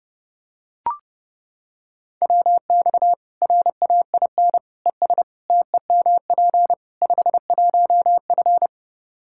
Morse code, heard as a keyed tone, about 32 wpm, 700 Hz. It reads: WX RAIN ES TEMP 51F
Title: Practice QSO - Exchange 2 - Info - 30wpm